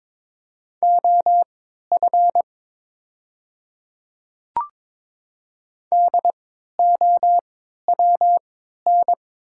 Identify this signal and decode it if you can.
Morse code, heard as a keyed tone, about 22 wpm, spaced as for 12 wpm, 700 Hz. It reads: OF DOWN
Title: 3x Character Spacing - Top 100 Words - 22wpm